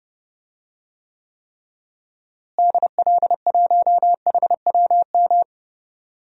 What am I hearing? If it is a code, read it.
Morse code, heard as a keyed tone, about 32 wpm, 700 Hz. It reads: DL1HWM
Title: DXCC-100 Call Signs - 6 Characters - Repeated 3x - 30wpm